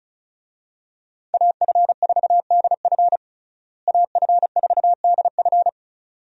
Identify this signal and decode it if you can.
Morse code, heard as a keyed tone, about 36 wpm, 700 Hz. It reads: AF4DF AF4DF
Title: USA Call Signs - 5 Characters - Repeated 2x - 35wpm